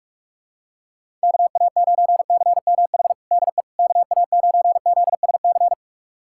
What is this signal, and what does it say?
Morse code, heard as a keyed tone, about 44 wpm, 700 Hz. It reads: KA9CMH DE KA9ZSC
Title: USA Callsign DE Callsign 45wpm